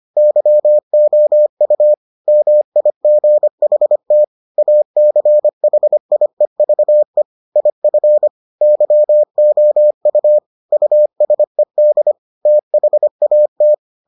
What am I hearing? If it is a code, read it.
Morse code, heard as a keyed tone, about 25 wpm, 600 Hz. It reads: YOU MIGHT ACHIEVE IF YOU USED THAT